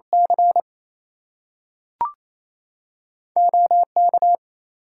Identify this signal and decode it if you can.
Morse code, heard as a keyed tone, about 28 wpm, 700 Hz. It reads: C OK
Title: US States and Canadian Provinces 28wpm